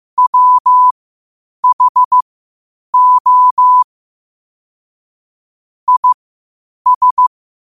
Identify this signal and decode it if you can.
Morse code, heard as a keyed tone, about 15 wpm, spaced as for 8 wpm, 1000 Hz. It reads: WHO IS
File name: qotdfiles / 8wpm